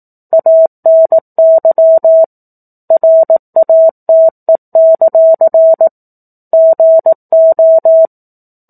Morse code, heard as a keyed tone, about 18 wpm, 650 Hz. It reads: ANY RATE; GO
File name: 04-Chapter-04 18 WPM